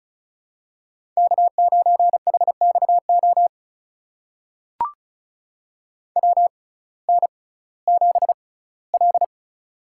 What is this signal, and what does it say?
Morse code, heard as a keyed tone, about 34 wpm, spaced as for 19 wpm, 700 Hz. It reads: K9HXO WN7L
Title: USA Call Signs - 6x Character Spacing - 35wpm